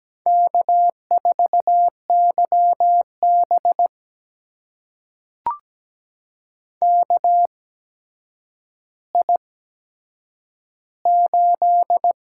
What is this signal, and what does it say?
Morse code, heard as a keyed tone, about 17 wpm, spaced as for 8 wpm, 700 Hz. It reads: K4YB KI8
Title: USA Call Signs - 8x Character Spacing - 17wpm